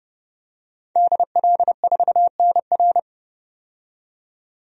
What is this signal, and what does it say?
Morse code, heard as a keyed tone, about 32 wpm, 700 Hz. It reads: DL4NR
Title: DXCC-100 Call Signs - 5 Characters - 30wpm